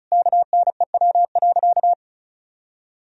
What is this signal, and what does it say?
Morse code, heard as a keyed tone, about 36 wpm, 700 Hz. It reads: KNEW.
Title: Sentences from Top 400 Words and Abbreviations 35wpm